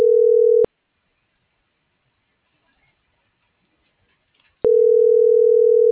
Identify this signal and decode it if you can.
Ringback tone (North America), 440 + 480 Hz.